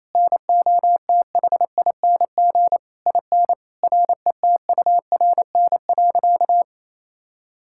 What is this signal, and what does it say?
Morse code, heard as a keyed tone, about 28 wpm, 700 Hz. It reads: NOTHING IN RETURN.
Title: Sentences from Top 1000 Words - Review Part 6 - 28wpm